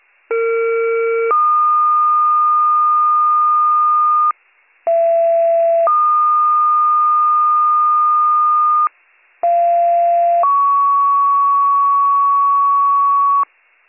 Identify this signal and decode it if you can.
Two-tone page, heard as a one-second tone, then a three-second tone, about 460 then 1220 Hz, about 670 then 1220 Hz, about 690 then 1120 Hz. It is heard from 0.3 to 13.4 s.